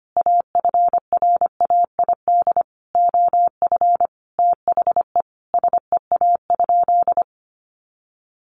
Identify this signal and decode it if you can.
Morse code, heard as a keyed tone, about 25 wpm, 700 Hz. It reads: AFRAID OF THE SEA?
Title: Sentences from Top 1500 Words 25wpm